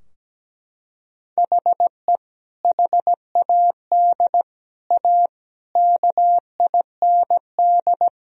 Morse code, heard as a keyed tone, about 17 wpm, 700 Hz. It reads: HE HAD A KIND
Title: Sentences from Top 1000 Words - Review Part 2 - 17wpm